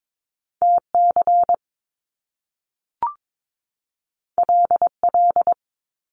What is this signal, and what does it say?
Morse code, heard as a keyed tone, about 22 wpm, 700 Hz. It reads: T C L L
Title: Top Two Letter Combinations in Words 22wpm